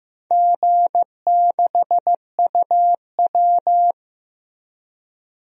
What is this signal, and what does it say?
Morse code, heard as a keyed tone, about 15 wpm, 700 Hz. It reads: G6UW
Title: DXCC-100 Call Signs - 4 Characters - Repeated 2x - 15wpm